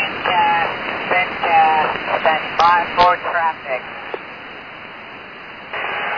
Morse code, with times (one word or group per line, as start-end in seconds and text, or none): none